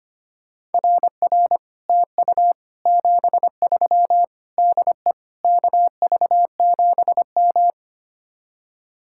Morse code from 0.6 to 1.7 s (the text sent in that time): RR